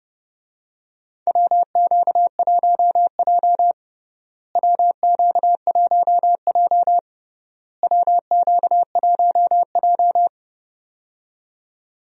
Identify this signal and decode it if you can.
Morse code, heard as a keyed tone, about 30 wpm, 700 Hz. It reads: WQ1J WQ1J WQ1J
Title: USA Call Signs - 4 Characters - Repeated 3x - 30wpm